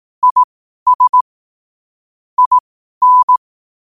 Morse code, heard as a keyed tone, about 18 wpm, spaced as for 12 wpm, 1000 Hz. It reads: IS IN
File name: qotdfiles / 12wpm